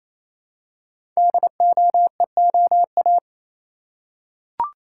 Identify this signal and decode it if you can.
Morse code, heard as a keyed tone, about 29 wpm, 700 Hz. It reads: DOEOA
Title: Code Groups - 5 Letters - 28wpm